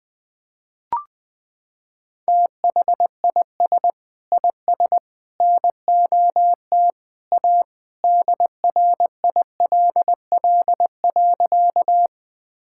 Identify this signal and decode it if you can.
Morse code, heard as a keyed tone, about 20 wpm, 700 Hz. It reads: THIS IS NOT A DRILL.
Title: Sentences with 3-Word Phrases - 20wpm